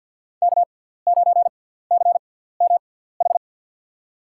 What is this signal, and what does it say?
Morse code, heard as a keyed tone, about 50 wpm, spaced as for 17 wpm, 700 Hz. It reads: K9CMH
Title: USA Call Signs - 5 Characters - 6x Character Spacing - 50wpm